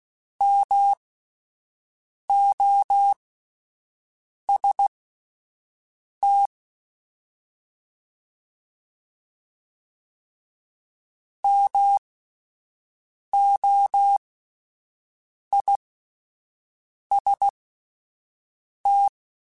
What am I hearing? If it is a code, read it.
Morse code, heard as a keyed tone, about 16 wpm, spaced as for 5 wpm, 800 Hz. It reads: MOST MOIST
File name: Alphabetical / Alphabetical mp3s / Ms @5